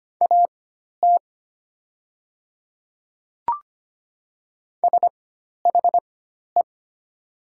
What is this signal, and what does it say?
Morse code, heard as a keyed tone, about 25 wpm, spaced as for 12 wpm, 700 Hz. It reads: AT SHE
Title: 4x Character Spacing - Top 100 Words - 25wpm